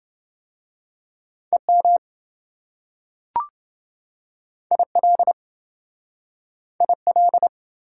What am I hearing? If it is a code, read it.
Morse code, heard as a keyed tone, about 27 wpm, 700 Hz. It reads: EM IL IL